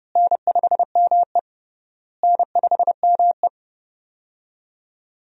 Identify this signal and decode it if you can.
Morse code, heard as a keyed tone, about 30 wpm, 700 Hz. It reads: N5ME N5ME
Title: USA Call Signs - 4 Characters - Repeated 3x - 30wpm